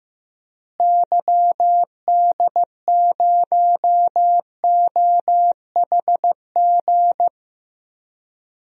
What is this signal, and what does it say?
Morse code, heard as a keyed tone, about 15 wpm, 700 Hz. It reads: YD0OHG
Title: DXCC-100 Call Signs - 6 Characters - Repeated 2x - 15wpm